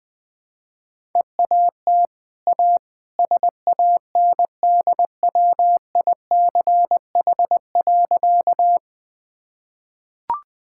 Morse code, heard as a keyed tone, about 20 wpm, 700 Hz. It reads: EAT A SANDWICH.
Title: Easy Sentences with Hints - 20wpm